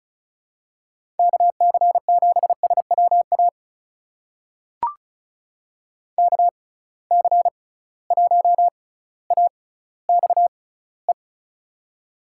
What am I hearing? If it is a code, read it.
Morse code, heard as a keyed tone, about 36 wpm, spaced as for 18 wpm, 700 Hz. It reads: KC7SWA KC1AXE
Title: USA Call Signs - 6 Characters - 6x Character Spacing - 35wpm